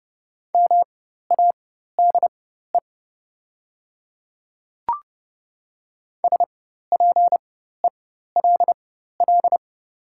Morse code, heard as a keyed tone, about 30 wpm, spaced as for 14 wpm, 700 Hz. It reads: MADE SPELL
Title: 4x Character Spacing - Top 200 Words - 30wpm